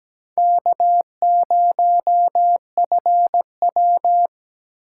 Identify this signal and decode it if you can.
Morse code, heard as a keyed tone, about 20 wpm, 700 Hz. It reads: K0FW
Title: Speed Racing - USA Call Signs 17wpm